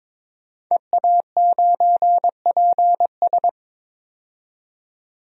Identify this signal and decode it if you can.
Morse code, heard as a keyed tone, about 22 wpm, 700 Hz. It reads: EA9PS
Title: DXCC-100 Call Signs - 5 Characters - 22wpm